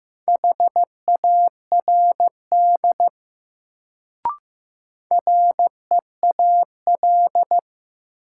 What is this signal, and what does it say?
Morse code, heard as a keyed tone, about 15 wpm, 700 Hz. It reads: HARD REAL
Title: Sentences from Top 300 Words - New Word Drill - 15wpm